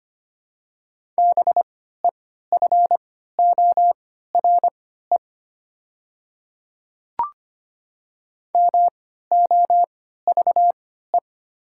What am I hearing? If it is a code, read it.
Morse code, heard as a keyed tone, about 25 wpm, spaced as for 14 wpm, 700 Hz. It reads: BEFORE MOVE
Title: 3x Character Spacing - Top 200 Words - 25wpm